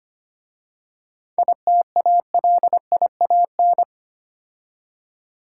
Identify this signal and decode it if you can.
Morse code, heard as a keyed tone, about 30 wpm, 700 Hz. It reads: ITALIAN
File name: Easy Words - Speed-Racing with Hints - 25wpm